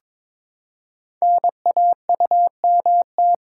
Morse code, heard as a keyed tone, about 23 wpm, 700 Hz. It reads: NAUMT
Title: Code Groups - 5 Letters - 22wpm